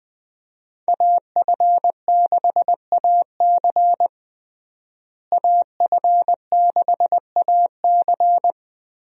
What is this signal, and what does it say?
Morse code, heard as a keyed tone, about 20 wpm, 700 Hz. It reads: AF6AC AF6AC
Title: USA Call Signs - 5 Characters - Repeated 3x - 20wpm